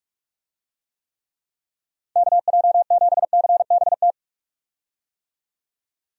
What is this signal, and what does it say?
Morse code, heard as a keyed tone, about 46 wpm, 700 Hz. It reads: KJ7CBT
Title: USA Call Signs - 6 Characters - 45wpm